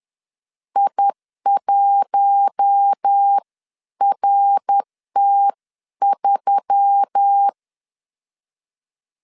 Morse code, heard as a keyed tone, about 10 wpm, spaced as for 9 wpm, 800 Hz. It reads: I1RT3